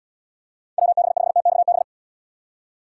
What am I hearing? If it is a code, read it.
Morse code, heard as a keyed tone, about 79 wpm, 690 Hz. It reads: WB5TXB